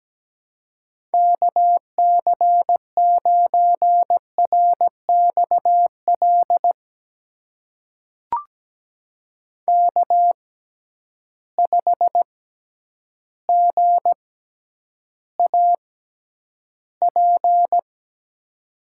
Morse code, heard as a keyed tone, about 17 wpm, spaced as for 9 wpm, 700 Hz. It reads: KC9RXL K5GAP
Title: USA Call Signs - 6x Character Spacing - 17wpm